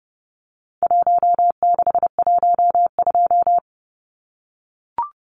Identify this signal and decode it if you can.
Morse code, heard as a keyed tone, about 30 wpm, 700 Hz. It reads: 1612
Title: Numbers - Contest Style - 4-Digit 30wpm